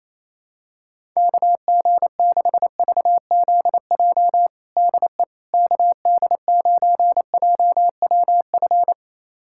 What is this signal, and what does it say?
Morse code, heard as a keyed tone, about 28 wpm, 700 Hz. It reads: KG6VZJ DE KD9JWF
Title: USA Callsign DE Callsign - 6 Characters - 28wpm